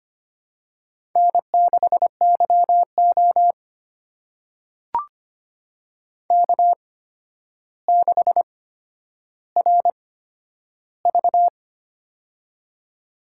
Morse code, heard as a keyed tone, about 25 wpm, spaced as for 11 wpm, 700 Hz. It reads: N6YO K6RV
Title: USA Call Signs - 4 Characters - 8x Character Spacing - 25wpm